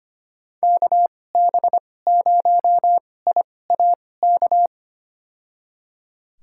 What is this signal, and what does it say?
Morse code, heard as a keyed tone, about 25 wpm, spaced as for 18 wpm, 700 Hz. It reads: KB0IAK
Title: USA Call Signs - 6 Characters - 2x Character Spacing - 25wpm